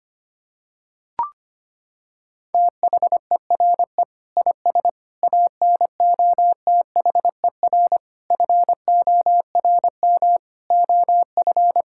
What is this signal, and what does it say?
Morse code, heard as a keyed tone, about 25 wpm, 700 Hz. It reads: THERE IS ANOTHER FORM OF
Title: Sentences from Top 1000 Words - Review Part 6 - 25wpm